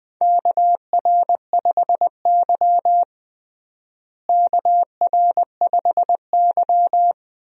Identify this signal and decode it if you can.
Morse code, heard as a keyed tone, about 20 wpm, 700 Hz. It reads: KR5Y KR5Y
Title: USA Call Signs - 4 Characters - Repeated 2x - 20wpm